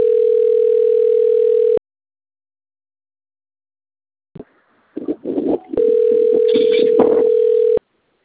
Ringback tone (North America), 440 + 480 Hz.